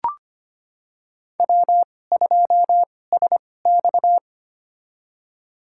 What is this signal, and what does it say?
Morse code, heard as a keyed tone, about 25 wpm, spaced as for 18 wpm, 700 Hz. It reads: W2SX
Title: USA Call Signs - 4 Characters - 2x Character Spacing - 25wpm